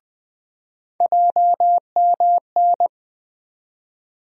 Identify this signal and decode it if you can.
Morse code, heard as a keyed tone, about 20 wpm, 700 Hz. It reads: JMN
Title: USA Call Sign Postfixes - 3 Letters 20wpm